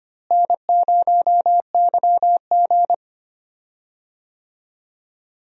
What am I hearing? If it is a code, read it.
Morse code, heard as a keyed tone, about 25 wpm, 700 Hz. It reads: N0YG
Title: USA Call Signs - 4 Characters - Repeated 2x - 25wpm